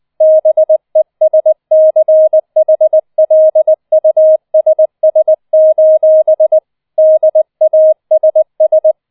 Morse code, heard as a keyed tone, about 20 wpm, 630 Hz. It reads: BESCHLUSS: DASS
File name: 004 Vierter Streich 20wpm